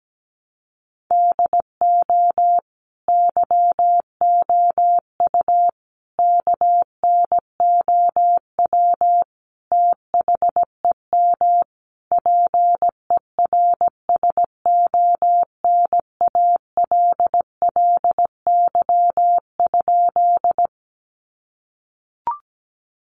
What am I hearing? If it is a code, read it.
Morse code, heard as a keyed tone, about 17 wpm, 700 Hz. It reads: DO YOU KNOW THEM PERSONALLY?
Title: Sentences from Top 1400 Words 17wpm